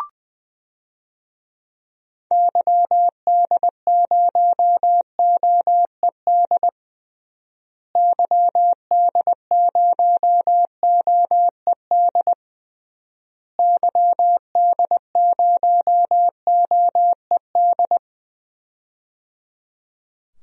Morse code, heard as a keyed tone, about 21 wpm, 700 Hz. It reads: YD0OED YD0OED YD0OED
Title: DXCC-100 Call Signs - 6 Characters - Repeated 3x - 20wpm